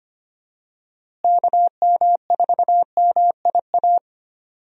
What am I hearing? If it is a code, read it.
Morse code, heard as a keyed tone, about 25 wpm, 700 Hz. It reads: KM4MIA
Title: USA Call Signs - 6 Characters - Repeated 3x - 25wpm